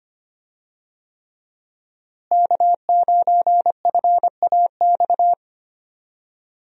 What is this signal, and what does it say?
Morse code, heard as a keyed tone, about 25 wpm, 700 Hz. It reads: K9FAX
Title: USA Call Signs - 5 Characters - Repeated 3x - 25wpm